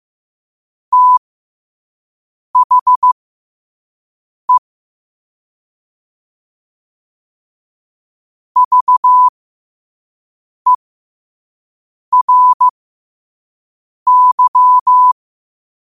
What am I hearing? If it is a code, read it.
Morse code, heard as a keyed tone, about 15 wpm, spaced as for 5 wpm, 1000 Hz. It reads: THE VERY
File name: qotdfiles / 5wpm